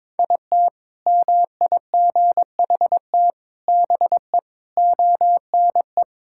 Morse code, heard as a keyed tone, about 22 wpm, 700 Hz. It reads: IT MIGHT BE ONE
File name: practice / 22wpm / Sentences with 4-Word Phrases - 22wpm